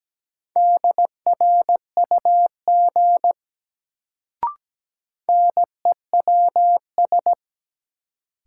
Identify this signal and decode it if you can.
Morse code, heard as a keyed tone, about 17 wpm, 700 Hz. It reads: DRUG NEWS
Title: Sets of 5 Words - New Word Drill - 17wpm